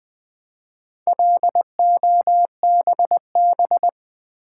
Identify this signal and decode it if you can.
Morse code, heard as a keyed tone, about 20 wpm, 700 Hz. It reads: LOBB
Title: Build A Word - 1 Letter At A Time - 3 to 5-Letter Words - 20wpm